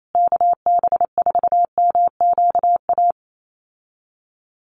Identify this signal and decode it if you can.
Morse code, heard as a keyed tone, about 28 wpm, 700 Hz. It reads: KB4MQA
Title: USA Call Signs Repeated 3x 28wpm